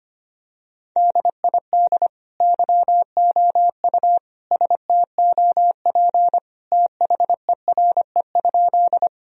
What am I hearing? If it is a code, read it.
Morse code, heard as a keyed tone, about 25 wpm, 700 Hz. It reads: DID YOU STOP THERE?
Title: Sentences from Top 1000 Words - Review Part 1 - 25wpm